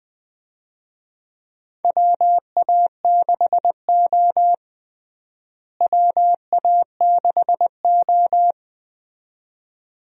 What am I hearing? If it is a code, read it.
Morse code, heard as a keyed tone, about 20 wpm, 700 Hz. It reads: WA6O WA6O
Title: USA Call Signs - 4 Characters - Repeated 2x - 20wpm